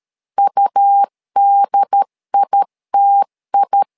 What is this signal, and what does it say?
Morse code, heard as a keyed tone, about 13 wpm, 800 Hz. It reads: UDITI
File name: LEZIONI / 2 UHDN / 2 UHD 12wpm 1540